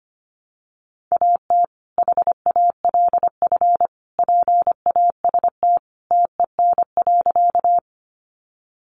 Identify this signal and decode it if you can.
Morse code, heard as a keyed tone, about 25 wpm, 700 Hz. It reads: AT HALF PAST TEN.
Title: Sentences from Top 600 Words 25wpm